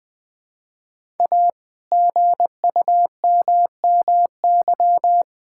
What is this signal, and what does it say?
Morse code, heard as a keyed tone, about 20 wpm, 700 Hz. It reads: A GUMMY
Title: All-Age Appropriate Jokes - 20wpm